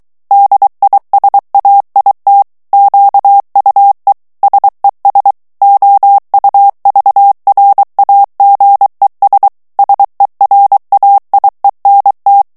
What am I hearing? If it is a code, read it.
Morse code, heard as a keyed tone, about 23 wpm, 800 Hz. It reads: DISAIT QUE SES OUVRAGES SERAIENT
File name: FRANCAIS TEXTE CLAIR 2 VITESSE 22 MOTS MINUTES